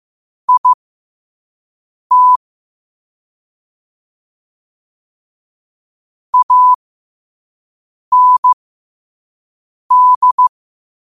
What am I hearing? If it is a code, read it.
Morse code, heard as a keyed tone, about 15 wpm, spaced as for 5 wpm, 1000 Hz. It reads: IT AND
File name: 5wpm